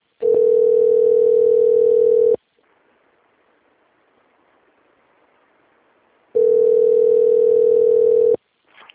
Ringback tone (North America), 440 + 480 Hz.
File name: rg-1600-unknown- - - .87039